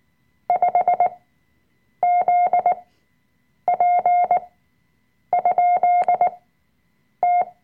Morse code, heard as a keyed tone, about 19 wpm, spaced as for 7 wpm, 710 Hz. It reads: 5ZP?T
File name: Workshop Dateien / CW-Online-Kurs / Lektion 005